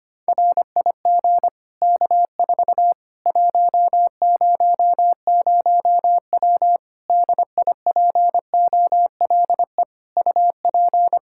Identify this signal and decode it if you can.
Morse code, heard as a keyed tone, about 25 wpm, 700 Hz. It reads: RIG K4 100W DIPOLE UP
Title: Practice QSO - Exchange 2 - Info - 25wpm